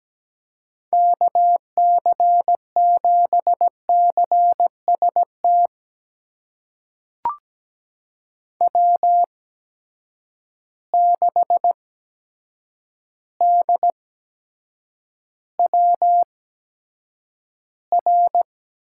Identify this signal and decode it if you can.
Morse code, heard as a keyed tone, about 17 wpm, spaced as for 8 wpm, 700 Hz. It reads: KC7CST W6DWR
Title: USA Call Signs - 8x Character Spacing - 17wpm